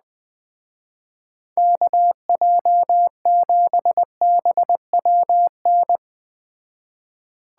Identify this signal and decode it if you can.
Morse code, heard as a keyed tone, about 20 wpm, 700 Hz. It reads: KJ7BWN